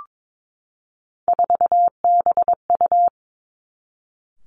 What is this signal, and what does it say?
Morse code, heard as a keyed tone, about 22 wpm, 700 Hz. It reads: 4BU
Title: Three Challenging Letter-Number Combinations 22wpm